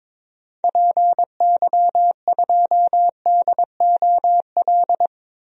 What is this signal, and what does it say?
Morse code, heard as a keyed tone, about 22 wpm, 700 Hz. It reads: PY2DOL
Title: DXCC-100 Call Signs - 6 Characters - Repeated 2x - 22wpm